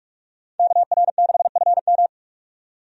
Morse code, heard as a keyed tone, about 42 wpm, 700 Hz. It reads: KR6FM
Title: USA Call Signs - 5 Characters - 45wpm